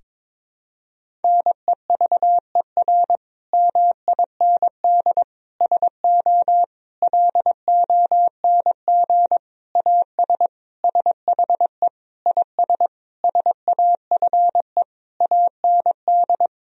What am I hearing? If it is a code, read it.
Morse code, heard as a keyed tone, about 22 wpm, 700 Hz. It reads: NEVER MIND SO LONG AS SHE IS SAFE AND
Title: Sentences from Top 1000 Words - Review Part 1 - 22wpm